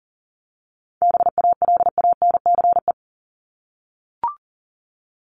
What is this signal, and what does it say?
Morse code, heard as a keyed tone, about 40 wpm, 700 Hz. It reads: BALANCE
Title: Top 751-1000 Words 40wpm